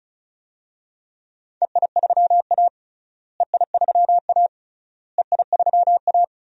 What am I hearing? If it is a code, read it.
Morse code, heard as a keyed tone, about 36 wpm, 700 Hz. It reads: EI3A EI3A EI3A
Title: DXCC-100 Call Signs - 4 Characters - Repeated 3x - 35wpm